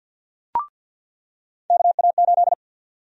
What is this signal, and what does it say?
Morse code, heard as a keyed tone, about 46 wpm, 700 Hz. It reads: KA8
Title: USA Call Sign Prefixes - 50wpm